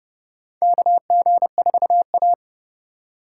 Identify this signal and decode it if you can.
Morse code, heard as a keyed tone, about 29 wpm, 700 Hz. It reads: KG4A
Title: USA Call Signs - 4 Characters - 2x Character Spacing - 30wpm